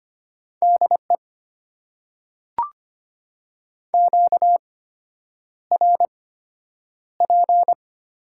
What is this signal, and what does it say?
Morse code, heard as a keyed tone, about 25 wpm, spaced as for 9 wpm, 700 Hz. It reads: DE QRP
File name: Top 90 QSO Elements - 8x Character Spacing - 25wpm